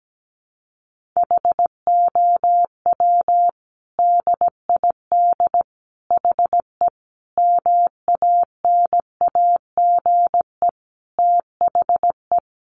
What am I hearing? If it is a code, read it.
Morse code, heard as a keyed tone, about 17 wpm, 700 Hz. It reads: HOW DID HE MANAGE THE
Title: Sentences from Top 1500 Words 17wpm